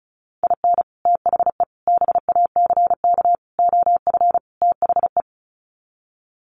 Morse code, heard as a keyed tone, about 34 wpm, 700 Hz. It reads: IN THE BACK OF THE
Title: Sets of 5 Words 35wpm